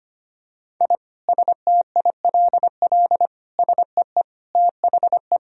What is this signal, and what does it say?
Morse code, heard as a keyed tone, about 25 wpm, 700 Hz. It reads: I STILL SEE THE
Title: Sentences from Top 1000 Words - Review Part 1 - 25wpm